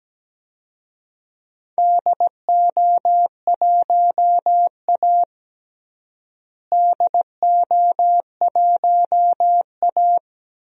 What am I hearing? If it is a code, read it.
Morse code, heard as a keyed tone, about 17 wpm, 700 Hz. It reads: DO1A DO1A
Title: DXCC-100 Call Signs - 4 Characters - Repeated 3x - 17wpm